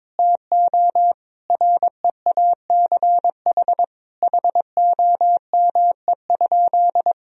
Morse code, heard as a keyed tone, about 22 wpm, 700 Hz. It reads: TO REACH HOME?
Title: Sentences from Top 1000 Words - Review Part 2 - 22wpm